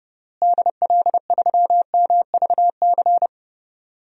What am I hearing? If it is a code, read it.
Morse code, heard as a keyed tone, about 29 wpm, 700 Hz. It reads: DL3MVC
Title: DXCC-100 Call Signs - 6 Characters - Repeated 3x - 30wpm